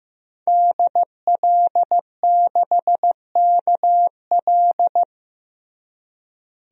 Morse code, heard as a keyed tone, about 15 wpm, 700 Hz. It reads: DL6KL
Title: DXCC-100 Call Signs - 5 Characters - Repeated 3x - 15wpm